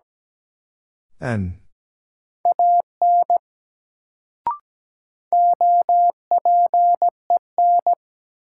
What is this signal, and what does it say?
Morse code, heard as a keyed tone, about 17 wpm, 700 Hz. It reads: AN OPEN
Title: Sentences from Top 300 Words - New Word Drill - 17wpm